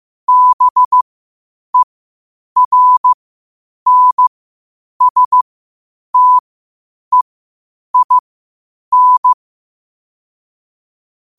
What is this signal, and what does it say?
Morse code, heard as a keyed tone, about 15 wpm, spaced as for 8 wpm, 1000 Hz. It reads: BERNSTEIN
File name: qotdfiles / 8wpm